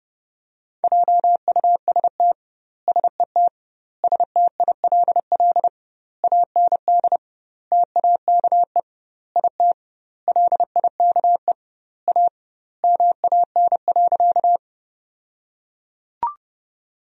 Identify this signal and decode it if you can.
Morse code, heard as a keyed tone, about 30 wpm, 700 Hz. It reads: JUST SET STILL AND TAKE IT LIKE A MAN.
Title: 2x Word-Spacing - Sentences from Top 300 Words - 30wpm